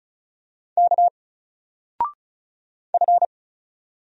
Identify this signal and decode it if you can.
Morse code, heard as a keyed tone, about 35 wpm, 700 Hz. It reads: K F
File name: Single-Letter-Number-35wpm